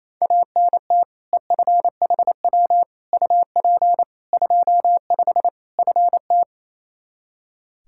Morse code, heard as a keyed tone, about 28 wpm, 700 Hz. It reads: ANT EFHW UP 25 FT
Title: K1USN SST QSO Practice 28wpm